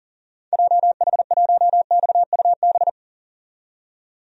Morse code, heard as a keyed tone, about 40 wpm, 700 Hz. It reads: JH1XUB
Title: DXCC-100 Call Signs - 6 Characters - 40wpm